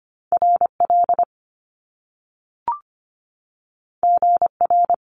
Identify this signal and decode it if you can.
Morse code, heard as a keyed tone, about 25 wpm, 700 Hz. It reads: RL GR
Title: Top Two Letter Combinations in Words 25wpm